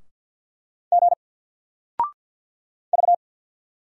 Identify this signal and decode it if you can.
Morse code, heard as a keyed tone, about 48 wpm, 700 Hz. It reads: G V
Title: Single Letter-Number Alt 50wpm